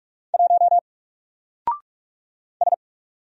Morse code, heard as a keyed tone, about 44 wpm, 700 Hz. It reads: 1 S
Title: Single Character (Letter:Number:Punctuation:Prosign) 45wpm